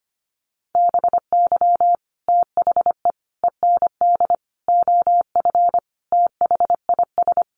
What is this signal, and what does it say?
Morse code, heard as a keyed tone, about 25 wpm, 700 Hz. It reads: BY THE END OF THIS
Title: Sets of 5 Words Encore 25wpm